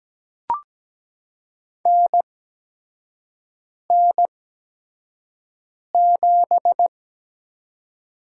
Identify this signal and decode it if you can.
Morse code, heard as a keyed tone, about 17 wpm, spaced as for 5 wpm, 700 Hz. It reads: NN7
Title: USA Call Signs - 4 Characters - 8x Character Spacing - 17wpm